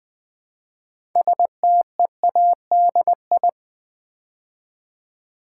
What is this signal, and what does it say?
Morse code, heard as a keyed tone, about 20 wpm, 700 Hz. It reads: STEADI
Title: Build A Word - 2 Letters At A Time - 8-Letter Words - 20wpm